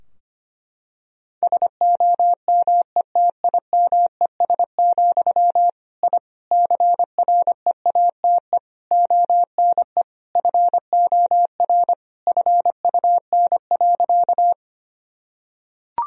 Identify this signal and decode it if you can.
Morse code, heard as a keyed tone, about 25 wpm, 700 Hz. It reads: SOMETIMES, I CREATE ONE FOR FUN.
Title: Sentences from Top 1000 Words - Review Part 3 - 25wpm